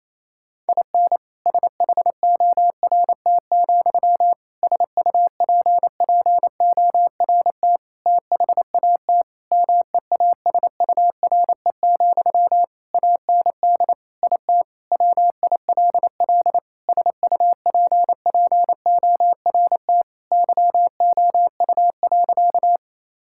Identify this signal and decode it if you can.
Morse code, heard as a keyed tone, about 27 wpm, 700 Hz. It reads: IN SHORT, SUPPORT THAT MEASURE, AND IT WILL SUPPORT YOU.
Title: Sentences from Top 1000 Words - Review Part 5 - 28wpm